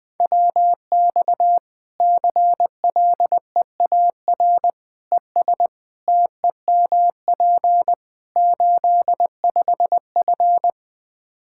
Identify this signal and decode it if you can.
Morse code, heard as a keyed tone, about 20 wpm, 700 Hz. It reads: WX CLEAR ES TEMP 85F
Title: Practice QSO - Exchange 2 - Info - 20wpm